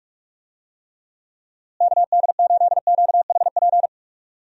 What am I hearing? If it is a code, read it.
Morse code, heard as a keyed tone, about 44 wpm, 700 Hz. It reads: KD8QHP
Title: USA Call Signs - 6 Characters - Repeated 2x - 45wpm